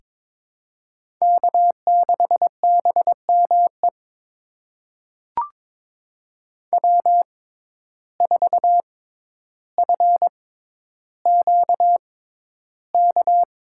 Morse code, heard as a keyed tone, about 22 wpm, spaced as for 11 wpm, 700 Hz. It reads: K6BME W4FQK
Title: USA Call Signs - 5 Characters - 6x Character Spacing - 22wpm